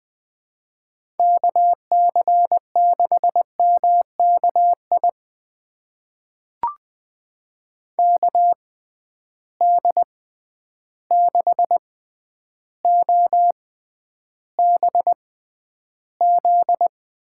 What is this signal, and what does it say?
Morse code, heard as a keyed tone, about 21 wpm, spaced as for 10 wpm, 700 Hz. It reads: KC6MKI KD6OBZ
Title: USA Call Signs - 6x Character Spacing - 20wpm